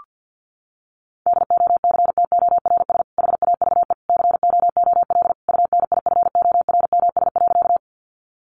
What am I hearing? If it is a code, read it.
Morse code, heard as a keyed tone, about 50 wpm, 700 Hz. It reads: DOCTORS HAVE COOL UNIFORMS.
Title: Easy Phrases 50wpm